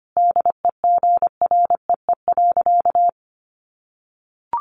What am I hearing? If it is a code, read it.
Morse code, heard as a keyed tone, about 25 wpm, 700 Hz. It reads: DEGREE.
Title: Sentences from Top 1500 Words 25wpm